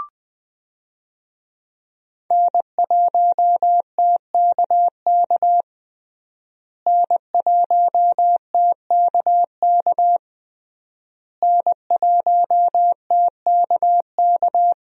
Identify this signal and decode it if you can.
Morse code, heard as a keyed tone, about 21 wpm, 700 Hz. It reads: N1TKK N1TKK N1TKK
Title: USA Call Signs - 5 Characters - Repeated 3x - 20wpm